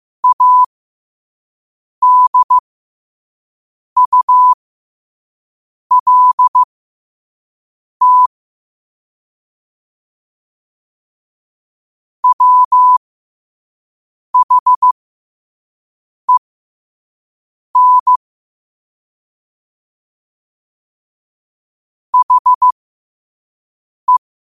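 Morse code, heard as a keyed tone, about 15 wpm, spaced as for 5 wpm, 1000 Hz. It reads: ADULT WHEN HE
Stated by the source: Morse code practice at 5 wpm.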